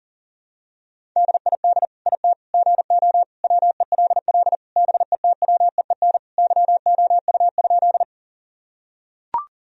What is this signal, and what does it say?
Morse code, heard as a keyed tone, about 37 wpm, 700 Hz. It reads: DID IT GO WELL BETWEEN YOU?
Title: Sentences from Top 1000 Words - Review Part 1 - 40wpm